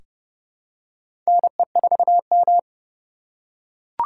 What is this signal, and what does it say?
Morse code, heard as a keyed tone, about 32 wpm, 700 Hz. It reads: NE4M
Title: K1USN SST QSO Practice - 30wpm